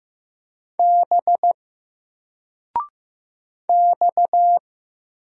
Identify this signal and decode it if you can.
Morse code, heard as a keyed tone, about 15 wpm, 700 Hz. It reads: B X
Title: Single Character (Letter:Number:Punctuation:Prosign) 15wpm